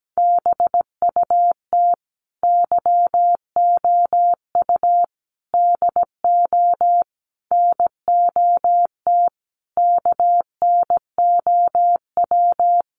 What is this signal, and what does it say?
Morse code, heard as a keyed tone, about 17 wpm, 700 Hz. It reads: BUT YOU DO NOT KNOW
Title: Sentences from Top 600 Words 17wpm